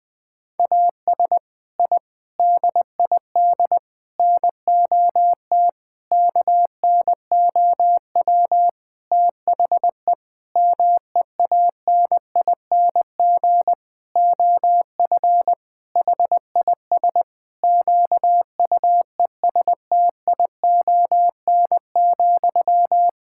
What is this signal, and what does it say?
Morse code, heard as a keyed tone, about 20 wpm, 700 Hz. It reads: AS I DID NOT KNOW THE MEANING OF HIS QUESTION,
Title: Sentences from Top 1000 Words - Review Part 6 - 20wpm